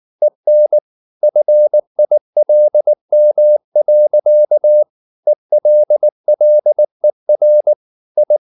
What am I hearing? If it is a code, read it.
Morse code, heard as a keyed tone, about 19 wpm, 600 Hz. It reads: EN FILM. ELLER I
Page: Ferie 19 wpm